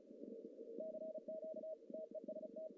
Morse code, heard as a keyed tone, about 32 wpm, 630 Hz, with noise. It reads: CQ TEST
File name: CW / cqww-cw-k5tr